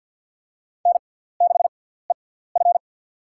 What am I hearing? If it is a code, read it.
Morse code, heard as a keyed tone, about 53 wpm, spaced as for 17 wpm, 700 Hz. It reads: N6EF
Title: USA Call Signs - 4 Characters - 6x Character Spacing - 50wpm